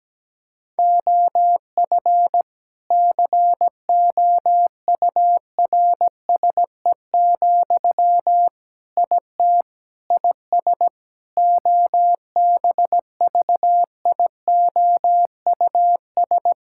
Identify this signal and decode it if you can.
Morse code, heard as a keyed tone, about 17 wpm, 700 Hz. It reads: OF COURSE, IT IS OBVIOUS
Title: Sentences from Top 1600 Words 17wpm